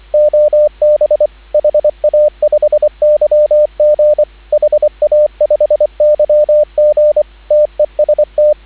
Morse code, heard as a keyed tone, about 25 wpm, 590 Hz, with radio noise. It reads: OB HA5YG HA5YG TEST